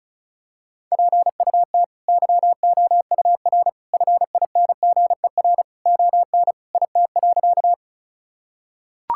Morse code, heard as a keyed tone, about 36 wpm, 700 Hz. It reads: PUT YOUR FINGER ON IT.
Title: Sentences from Top 1600 Words 35wpm